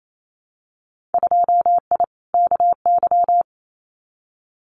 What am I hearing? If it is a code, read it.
Morse code, heard as a keyed tone, about 28 wpm, 700 Hz. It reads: 2I KY
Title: Winter Field Day Exchanges 28wpm